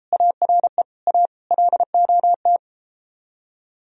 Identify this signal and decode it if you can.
Morse code, heard as a keyed tone, about 34 wpm, 700 Hz. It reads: ARE A LOT